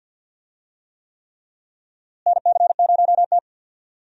Morse code, heard as a keyed tone, about 50 wpm, 700 Hz. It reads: NC0T
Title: USA Call Signs - 4 Characters - Repeated 3x - 50wpm